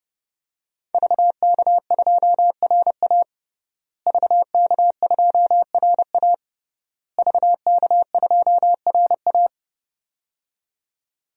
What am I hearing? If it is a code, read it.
Morse code, heard as a keyed tone, about 32 wpm, 700 Hz. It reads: VK2RA VK2RA VK2RA
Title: DXCC-100 Call Signs - 5 Characters - Repeated 3x - 30wpm